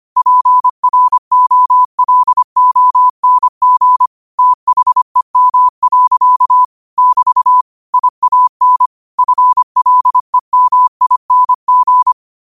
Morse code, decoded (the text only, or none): PROLONG THEM. <BT> IAN FLEMING